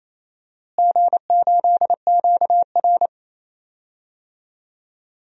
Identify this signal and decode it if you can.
Morse code, heard as a keyed tone, about 28 wpm, 700 Hz. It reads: G8QR
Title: DXCC-100 Call Signs - 4 Characters - Repeated 3x - 28wpm